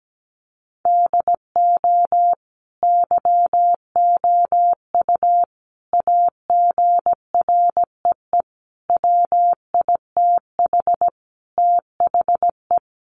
Morse code, heard as a keyed tone, about 17 wpm, 700 Hz. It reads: DO YOU AGREE WITH THE
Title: Sentences from Top 1400 Words 17wpm